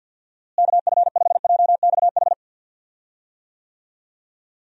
Morse code, heard as a keyed tone, about 48 wpm, 700 Hz. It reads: KF5JXH
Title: USA Call Signs - 6 Characters - Repeated 2x - 50wpm